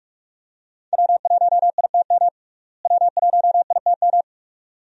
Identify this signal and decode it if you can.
Morse code, heard as a keyed tone, about 42 wpm, 700 Hz. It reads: W1ITM W1ITM
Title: USA Call Signs - 5 Characters - Repeated 2x - 45wpm